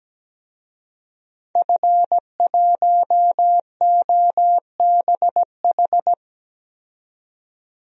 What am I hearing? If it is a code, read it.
Morse code, heard as a keyed tone, about 17 wpm, 700 Hz. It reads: F1OBH
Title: DXCC-100 Call Signs - 5 Characters - 17wpm